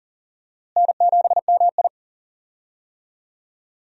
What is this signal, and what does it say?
Morse code, heard as a keyed tone, about 40 wpm, 700 Hz. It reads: N7MI
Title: USA Call Signs - 4 Characters - Repeated 2x - 40wpm